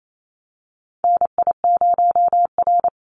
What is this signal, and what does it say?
Morse code, heard as a keyed tone, about 28 wpm, 700 Hz. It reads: NI0R
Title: Call Signs - 28wpm